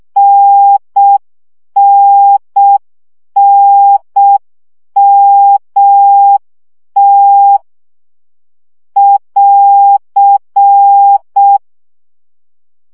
Morse code, heard as a keyed tone, about 6 wpm, 800 Hz. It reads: NNNMT <AR>
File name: CW gyakorlas t n m 5 baud